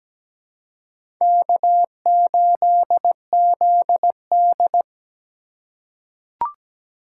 Morse code, heard as a keyed tone, about 17 wpm, 700 Hz. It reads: K8ZD
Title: USA Call Signs - 4 Characters - 4x Character Spacing - 17wpm